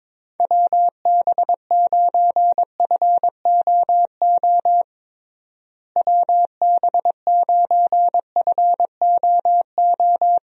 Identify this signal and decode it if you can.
Morse code, heard as a keyed tone, about 22 wpm, 700 Hz. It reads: WB9FOO WB9FOO
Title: USA Call Signs - 6 Characters - Repeated 3x - 22wpm